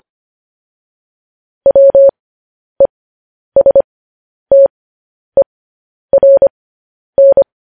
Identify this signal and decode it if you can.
Morse code, heard as a keyed tone, about 25 wpm, spaced as for 10 wpm, 550 Hz. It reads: WESTERN
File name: podcast / News headlines-1x-25x10